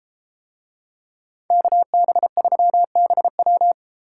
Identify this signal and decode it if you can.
Morse code, heard as a keyed tone, about 32 wpm, 700 Hz. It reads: KB3BW